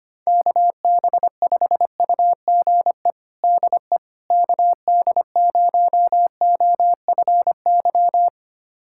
Morse code, heard as a keyed tone, about 25 wpm, 700 Hz. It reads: KB5UGE DE KD0OFY
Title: USA Callsign DE Callsign 25wpm